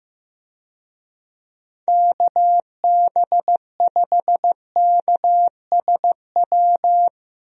Morse code, heard as a keyed tone, about 16 wpm, 700 Hz. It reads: KB5KSW
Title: USA Call Signs - 6 Characters - 15wpm